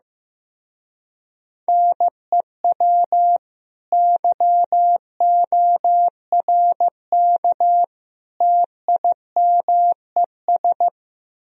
Morse code, heard as a keyed tone, about 15 wpm, 700 Hz. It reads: NEW YORK TIMES
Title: Speed-Racing - Sets of 3 Words 15wpm